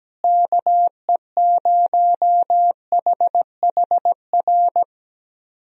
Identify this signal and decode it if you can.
Morse code, heard as a keyed tone, about 17 wpm, 700 Hz. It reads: KE0HHR
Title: USA Call Signs - 6 Characters - Repeated 3x - 17wpm